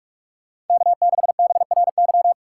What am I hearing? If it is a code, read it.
Morse code, heard as a keyed tone, about 44 wpm, 700 Hz. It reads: K6BRY